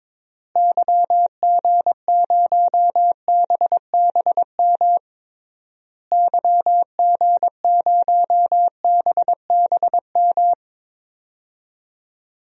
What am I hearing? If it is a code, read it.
Morse code, heard as a keyed tone, about 22 wpm, 700 Hz. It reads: YG0BBM YG0BBM
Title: DXCC-100 Call Signs - 6 Characters - Repeated 3x - 22wpm